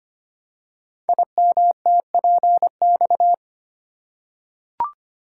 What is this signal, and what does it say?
Morse code, heard as a keyed tone, about 25 wpm, 700 Hz. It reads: IMTPX
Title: Code Groups - 5 Letters - 25wpm